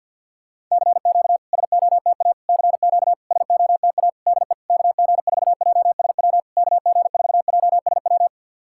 Morse code, heard as a keyed tone, about 51 wpm, 690 Hz. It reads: CQ SOTA CQ SOTA DE KG4JSW KG4JSW
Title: SOTA QSO Practice 50wpm